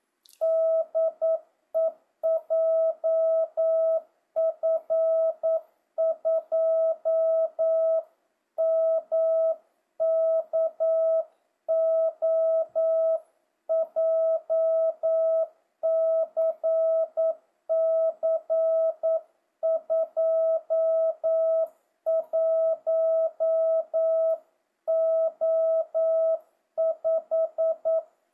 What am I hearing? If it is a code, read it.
Morse code, heard as a keyed tone, about 9 wpm, 650 Hz. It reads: DE JF2MKO JCC 21O5
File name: jh8cbh / 2025cw